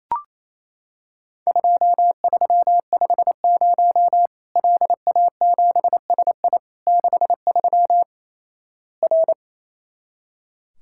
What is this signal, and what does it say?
Morse code, heard as a keyed tone, about 28 wpm, 700 Hz. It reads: 2350 LA7SI 63 R
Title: Worked All Europe DX Contest - QTC Exchange - 28wpm